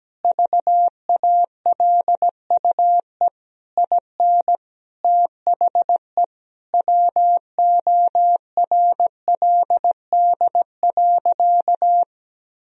Morse code, heard as a keyed tone, about 17 wpm, 700 Hz. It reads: VALUE IN THE WORLD.
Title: Sentences from Top 1000 Words - Review Part 3 - 17wpm